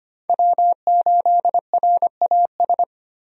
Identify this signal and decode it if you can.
Morse code, heard as a keyed tone, about 25 wpm, 690 Hz. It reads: W8RAS
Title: USA Call Signs - 5 Characters - 25wpm